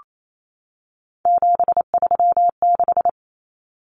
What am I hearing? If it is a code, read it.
Morse code, heard as a keyed tone, about 28 wpm, 700 Hz. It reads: 736
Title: Common Cut Numbers - 3-Digit 28wpm